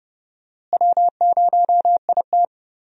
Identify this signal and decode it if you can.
Morse code, heard as a keyed tone, about 30 wpm, 700 Hz. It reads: W0IT
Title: USA Call Signs - 4 Characters - Repeated 2x - 30wpm